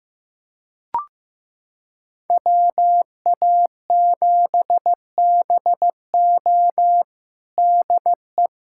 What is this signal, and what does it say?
Morse code, heard as a keyed tone, about 16 wpm, 700 Hz. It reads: WA7BO DE
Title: USA Callsign DE Callsign - 5 Characters - 15wpm